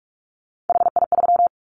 Morse code, heard as a keyed tone, about 44 wpm, 700 Hz. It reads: HI3
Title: Callsign Prefixes for Top 100 DXCC - 45wpm